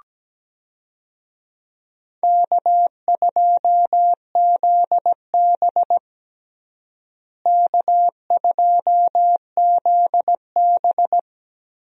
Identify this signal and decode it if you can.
Morse code, heard as a keyed tone, about 17 wpm, 700 Hz. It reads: K2ZB K2ZB
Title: USA Call Signs - 4 Characters - Repeated 3x - 17wpm